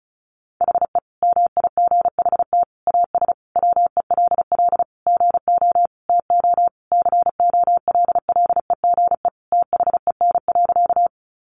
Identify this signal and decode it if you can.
Morse code, heard as a keyed tone, about 34 wpm, 700 Hz. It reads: HE MIGHT AS WELL GO TO COLLEGE THEN.
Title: Sentences from Top 1300 Words 35wpm